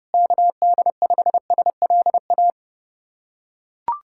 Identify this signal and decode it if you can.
Morse code, heard as a keyed tone, about 29 wpm, 700 Hz. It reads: KD5SLA
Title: USA Callsign DE Callsign - 6 Characters - 30wpm